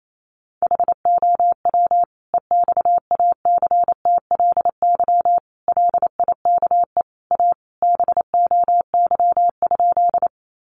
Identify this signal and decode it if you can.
Morse code, heard as a keyed tone, about 28 wpm, 700 Hz. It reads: HOW EXACTLY LIKE A BOY?
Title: Sentences from Top 900 Words 28wpm